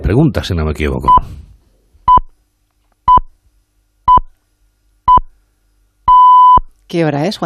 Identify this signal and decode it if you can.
Radio time signal (time pips), with speech. It is heard from 1.1 to 6.6 s.